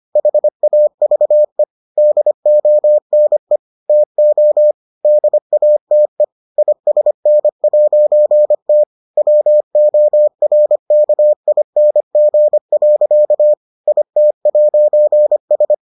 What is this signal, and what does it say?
Morse code, heard as a keyed tone, about 25 wpm, 600 Hz. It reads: HAVE DONE TO DATE ISN'T WORKING. IT'S